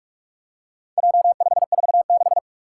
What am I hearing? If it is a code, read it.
Morse code, heard as a keyed tone, about 44 wpm, 700 Hz. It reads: J546
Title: Four Challenging Letter-Number Combinations 45wpm